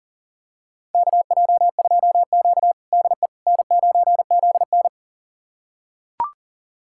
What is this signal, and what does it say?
Morse code, heard as a keyed tone, about 37 wpm, 700 Hz. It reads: KJ2Q DE N9ZN
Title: USA Callsign DE Callsign - 4 Characters - 40wpm